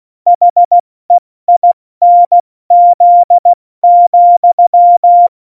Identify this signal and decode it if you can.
Morse code, heard as a keyed tone, about 16 wpm, 700 Hz. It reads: HEINZ,
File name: Morsekurs / Witze / Witz 18. Februar 2020, 70 F80